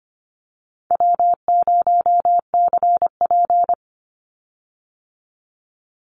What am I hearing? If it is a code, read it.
Morse code, heard as a keyed tone, about 25 wpm, 700 Hz. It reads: W0CP
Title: Easy Call Signs - 25wpm